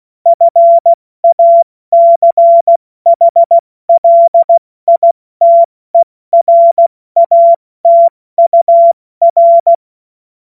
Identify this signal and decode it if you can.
Morse code, heard as a keyed tone, about 16 wpm, 680 Hz. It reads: FACHLITERATUR
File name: Morsekurs / Witze / Witz 21. Dezember 2021, 70 F80